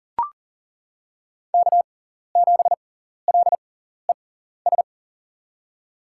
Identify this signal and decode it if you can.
Morse code, heard as a keyed tone, about 38 wpm, spaced as for 14 wpm, 700 Hz. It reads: K7LES
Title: USA Call Signs - 5 Characters - 6x Character Spacing - 40wpm